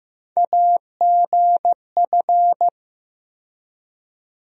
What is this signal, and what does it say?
Morse code, heard as a keyed tone, about 15 wpm, 700 Hz. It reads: AGF
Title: USA Call Sign Postfixes - 3 Letters 15wpm